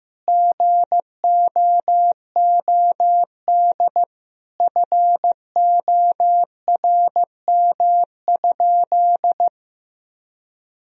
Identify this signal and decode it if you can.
Morse code, heard as a keyed tone, about 15 wpm, 700 Hz. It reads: GOOD FORM?
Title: Sentences from Top 1000 Words - Review Part 1 - 15wpm